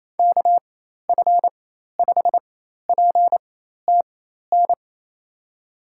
Morse code, heard as a keyed tone, about 28 wpm, spaced as for 13 wpm, 700 Hz. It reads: KF5PTN
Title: USA Call Signs - 4x Character Spacing - 28wpm